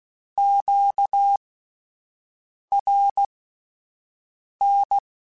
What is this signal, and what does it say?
Morse code, heard as a keyed tone, about 16 wpm, spaced as for 5 wpm, 800 Hz. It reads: QRN
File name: (spaced) / Q-Calls Spaced-mp3s / Q Calls sp Once @10